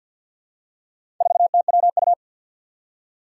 Morse code, heard as a keyed tone, about 50 wpm, 700 Hz. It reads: 4TWU
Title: Four Challenging Letter-Number Combinations 50wpm